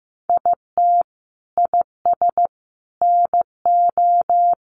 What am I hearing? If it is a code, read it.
Morse code, heard as a keyed tone, about 15 wpm, 700 Hz. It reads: IT IS NO
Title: Sentences from Top 1200 Words 15wpm